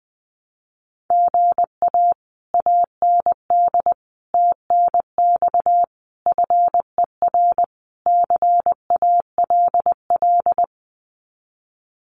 Morse code, heard as a keyed tone, about 19 wpm, 700 Hz. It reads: GA AND TNX FER CALL
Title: Common QSO Phrases Encore 20wpm